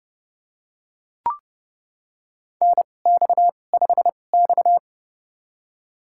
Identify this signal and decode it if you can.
Morse code, heard as a keyed tone, about 30 wpm, spaced as for 22 wpm, 700 Hz. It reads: NX5X
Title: USA Call Signs - 4 Characters - 2x Character Spacing - 30wpm